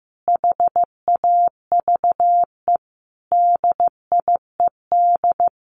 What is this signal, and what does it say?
Morse code, heard as a keyed tone, about 15 wpm, 700 Hz. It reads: HAVE DIED
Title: A Sound Of Thunder 15wpm